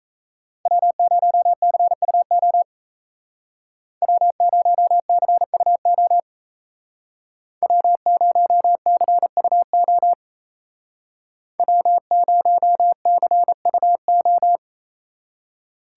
Morse code, heard as a keyed tone, about 36 wpm, 700 Hz. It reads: W0CUO W0CUO W0CUO W0CUO